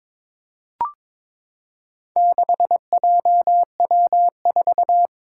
Morse code, heard as a keyed tone, about 21 wpm, 700 Hz. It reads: 6JW4
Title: Four Challenging Letter-Number Combinations 22wpm